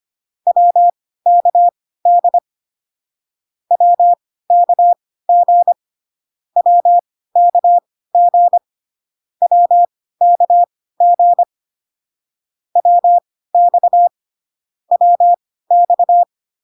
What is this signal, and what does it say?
Morse code, heard as a keyed tone, about 25 wpm, spaced as for 16 wpm, 700 Hz. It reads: WKD WKG WKG WKG WX WX